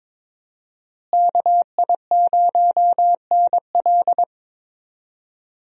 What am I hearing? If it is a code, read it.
Morse code, heard as a keyed tone, about 22 wpm, 700 Hz. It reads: KI0NL
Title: USA Call Signs - 5 Characters - Repeated 3x - 22wpm